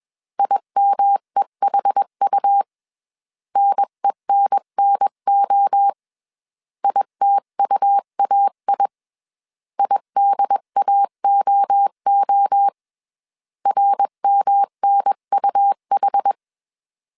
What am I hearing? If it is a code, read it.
Morse code, heard as a keyed tone, about 21 wpm, 800 Hz. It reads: IMEHU NENNO ITUAI IDAOO RMNUH